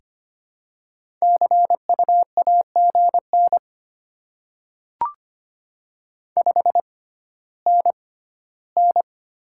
Morse code, heard as a keyed tone, about 25 wpm, spaced as for 14 wpm, 700 Hz. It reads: CU AGN 5NN
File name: Top 20 QSO Elements - 6x Character Spacing - 25wpm